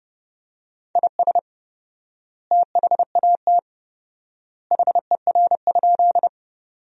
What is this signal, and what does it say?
Morse code, heard as a keyed tone, about 30 wpm, 700 Hz. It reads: IS THAT HER?
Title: 4x Word-Spacing - Sentences from Top 100 Words - 30wpm